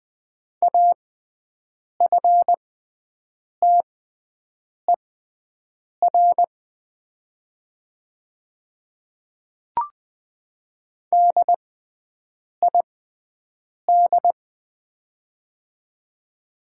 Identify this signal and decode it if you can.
Morse code, heard as a keyed tone, about 20 wpm, spaced as for 7 wpm, 700 Hz. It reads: AFTER DID
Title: 6x Character Spacing - Top 200 Words - 20wpm